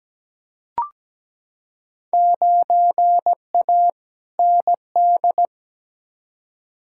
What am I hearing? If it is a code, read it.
Morse code, heard as a keyed tone, about 17 wpm, 700 Hz. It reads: 9A ND
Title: Field Day Exchanges Encore - 17wpm